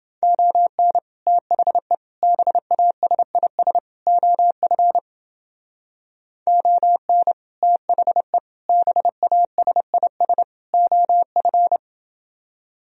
Morse code, heard as a keyed tone, about 29 wpm, 700 Hz. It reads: ON THE BASIS OF ON THE BASIS OF